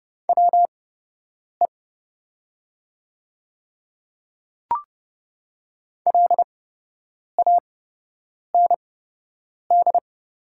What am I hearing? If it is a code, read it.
Morse code, heard as a keyed tone, about 30 wpm, spaced as for 8 wpm, 690 Hz. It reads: WE LAND
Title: 8x Character Spacing - Top 200 Words - 30wpm